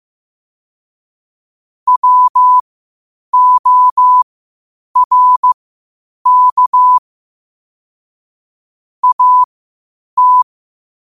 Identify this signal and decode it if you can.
Morse code, heard as a keyed tone, about 15 wpm, spaced as for 8 wpm, 1000 Hz. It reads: WORK AT
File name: qotdfiles / 8wpm